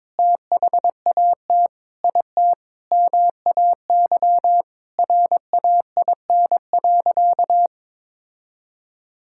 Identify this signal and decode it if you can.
Morse code, heard as a keyed tone, about 22 wpm, 700 Hz. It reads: THAT IT MAY RAIN.
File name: practice / 22wpm / Sentences with 8-Word Phrases - 22wpm